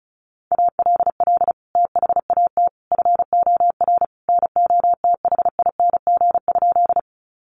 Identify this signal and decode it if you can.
Morse code, heard as a keyed tone, about 34 wpm, 700 Hz. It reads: ALL THAT FOR NOTHING?
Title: Sentences from Top 500 Words 35wpm